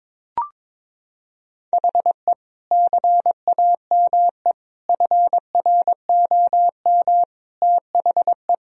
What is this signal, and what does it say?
Morse code, heard as a keyed tone, about 22 wpm, 700 Hz. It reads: HE CAME FROM THE
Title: Sentences from Top 1000 Words - Review Part 2 - 22wpm